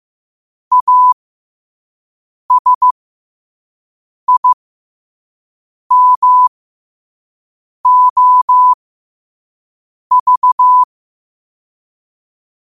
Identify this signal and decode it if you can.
Morse code, heard as a keyed tone, about 15 wpm, spaced as for 5 wpm, 1000 Hz. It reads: ASIMOV
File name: qotdfiles / 5wpm